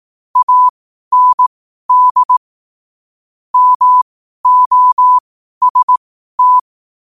Morse code, heard as a keyed tone, about 18 wpm, spaced as for 12 wpm, 1000 Hz. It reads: AND MOST